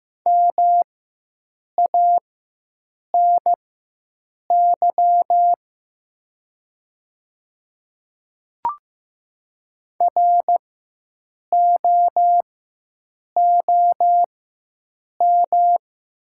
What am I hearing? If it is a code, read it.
Morse code, heard as a keyed tone, about 15 wpm, spaced as for 7 wpm, 700 Hz. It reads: MANY ROOM
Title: 4x Character Spacing - Top 500 Words - 15wpm